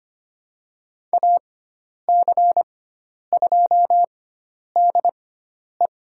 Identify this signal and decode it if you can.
Morse code, heard as a keyed tone, about 25 wpm, spaced as for 10 wpm, 700 Hz. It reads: AC2DE
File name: Lesson 16g - Letter c - Call Signs 25wpm